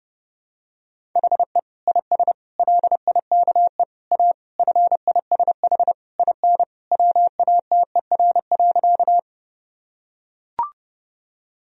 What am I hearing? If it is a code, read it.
Morse code, heard as a keyed tone, about 32 wpm, 700 Hz. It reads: HE IS LIKE A FISH IN WATER.
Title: Sentences from Top 1000 Words - Review Part 1 - 30wpm